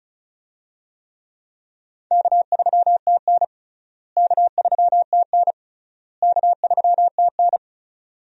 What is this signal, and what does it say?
Morse code, heard as a keyed tone, about 36 wpm, 700 Hz. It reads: K3TN K3TN K3TN
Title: USA Call Signs - 4 Characters - Repeated 3x - 35wpm